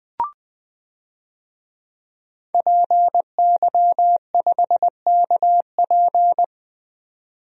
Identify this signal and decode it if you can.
Morse code, heard as a keyed tone, about 20 wpm, 700 Hz. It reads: PY5KP
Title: DXCC-100 Call Signs - 5 Characters - 20wpm